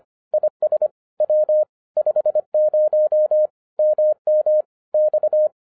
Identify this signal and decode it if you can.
Morse code, heard as a keyed tone, about 25 wpm, 610 Hz. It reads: IS W 50 MM X